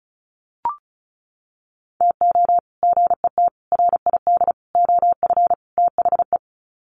Morse code, heard as a keyed tone, about 34 wpm, 700 Hz. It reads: TO GET RID OF THE
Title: Sets of 5 Words 35wpm